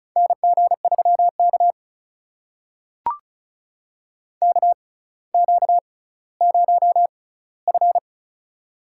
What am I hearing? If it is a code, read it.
Morse code, heard as a keyed tone, about 34 wpm, spaced as for 18 wpm, 700 Hz. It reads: NG3K KQ0F
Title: USA Call Signs - 4 Characters - 6x Character Spacing - 35wpm